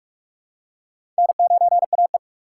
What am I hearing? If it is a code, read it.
Morse code, heard as a keyed tone, about 44 wpm, 700 Hz. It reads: N9AE
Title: USA Call Signs - 4 Characters - 3x Character Spacing - 45wpm